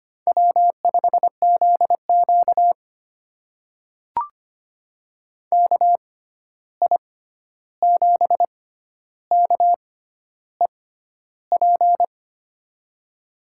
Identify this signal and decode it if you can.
Morse code, heard as a keyed tone, about 25 wpm, spaced as for 11 wpm, 700 Hz. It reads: W5ZQ KI7KEP
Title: USA Call Signs - 6x Character Spacing - 25wpm